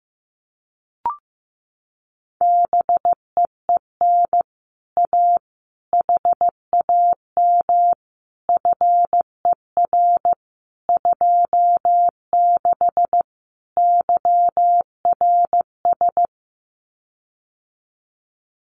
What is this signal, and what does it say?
Morse code, heard as a keyed tone, about 15 wpm, 700 Hz. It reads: BEEN A HAM FER 26 YRS
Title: Common QSO Phrases Encore 15wpm